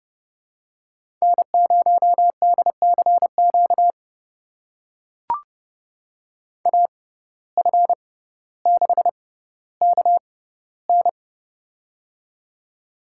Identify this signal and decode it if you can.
Morse code, heard as a keyed tone, about 30 wpm, spaced as for 15 wpm, 700 Hz. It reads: N0DCQ AF6KN
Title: USA Call Signs - 6x Character Spacing - 30wpm